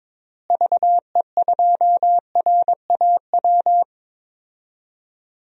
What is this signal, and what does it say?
Morse code, heard as a keyed tone, about 21 wpm, 700 Hz. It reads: VE2RAW
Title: DXCC-100 Call Signs - 6 Characters - Repeated 3x - 22wpm